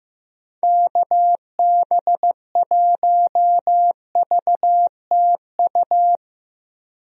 Morse code, heard as a keyed tone, about 15 wpm, 700 Hz. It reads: KB1VTU
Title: USA Call Signs - 6 Characters - Repeated 3x - 15wpm